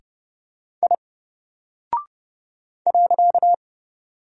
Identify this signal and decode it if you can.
Morse code, heard as a keyed tone, about 30 wpm, 700 Hz. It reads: I .
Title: Single Character (Letter:Number:Punctuation:Prosign) Alt 30wpm